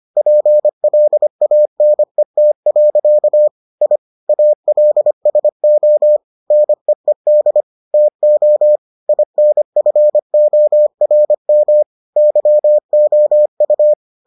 Morse code, heard as a keyed tone, about 25 wpm, 600 Hz. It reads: PLANET. I ALSO NEED TO INFORM YOU